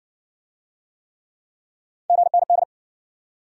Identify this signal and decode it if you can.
Morse code, heard as a keyed tone, about 55 wpm, 700 Hz. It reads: GND